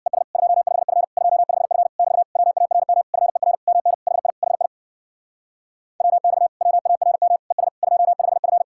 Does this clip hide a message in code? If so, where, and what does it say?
Yes: Morse code from 0.1 to 8.6 s, heard as a keyed tone, about 67 wpm, 700 Hz. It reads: ES 25F 25F <BT> WARM FER MN HI HI WX WARM ES 25F